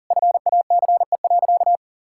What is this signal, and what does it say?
Morse code, heard as a keyed tone, about 40 wpm, 700 Hz. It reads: FACE.
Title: Sentences from Top 1600 Words 40wpm